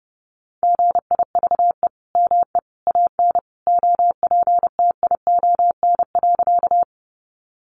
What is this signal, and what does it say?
Morse code, heard as a keyed tone, about 30 wpm, 700 Hz. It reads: GIVE ME AN OPTION.
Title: Sentences from Top 1300 Words 30wpm